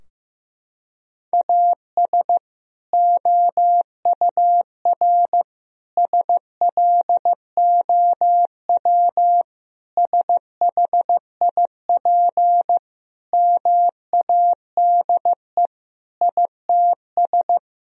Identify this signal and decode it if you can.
Morse code, heard as a keyed tone, about 15 wpm, 700 Hz. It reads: AS OUR SLOW SHIP MADE ITS
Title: Sentences from Top 1000 Words - Review Part 2 - 15wpm